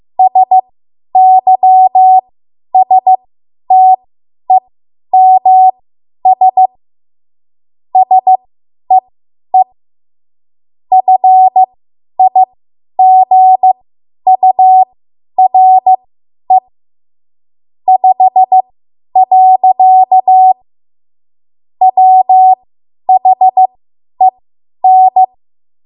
Morse code, heard as a keyed tone, about 15 wpm, spaced as for 10 wpm, 750 Hz. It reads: SYSTEMS SEE FIGURE 5. WHEN
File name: Archive / 10 WPM / 10WPM